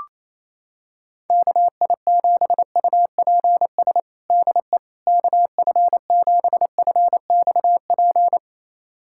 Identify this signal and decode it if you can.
Morse code, heard as a keyed tone, about 28 wpm, 700 Hz. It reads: KI7UPS DE KF7FXP
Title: USA Callsign DE Callsign - 6 Characters - 28wpm